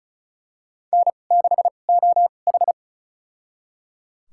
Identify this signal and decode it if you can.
Morse code, heard as a keyed tone, about 34 wpm, spaced as for 25 wpm, 690 Hz. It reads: N6OH
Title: USA Call Signs - 4 Characters - 2x Character Spacing - 35wpm